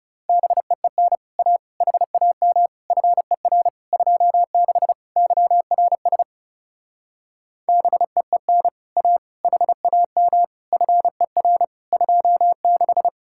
Morse code, heard as a keyed tone, about 38 wpm, 700 Hz. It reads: BEEN A HAM FER 26 YRS BEEN A HAM FER 26